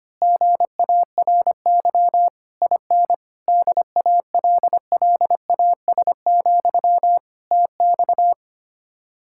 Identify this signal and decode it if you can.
Morse code, heard as a keyed tone, about 24 wpm, 700 Hz. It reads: GARY IN DALLAS, TX
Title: Speed-Racing Common QSO Phrases - 25wpm